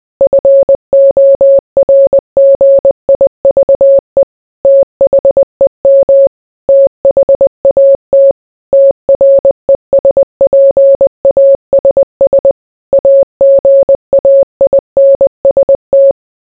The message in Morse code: FORGIVE THEM THAT TRESPASS AGAINST